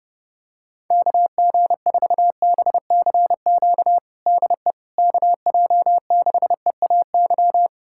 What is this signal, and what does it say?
Morse code, heard as a keyed tone, about 32 wpm, 700 Hz. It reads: KG4BCQ DE KJ6EAY
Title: USA Callsign DE Callsign - 6 Characters - 30wpm